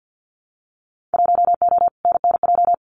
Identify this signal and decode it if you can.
Morse code, heard as a keyed tone, about 50 wpm, 700 Hz. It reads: 1O NNJ
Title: Winter Field Day Exchanges 50wpm